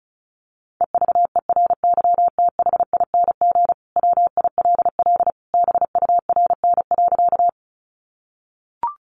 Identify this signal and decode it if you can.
Morse code, heard as a keyed tone, about 34 wpm, 700 Hz. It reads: EVERYTHING WILL BURN.
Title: Sentences from Top 1300 Words 35wpm